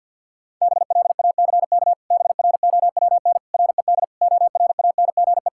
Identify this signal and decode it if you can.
Morse code, heard as a keyed tone, about 52 wpm, 700 Hz. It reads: BLACK BROWN RED ORANGE
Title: Pangram Challenge v2 - 50wpm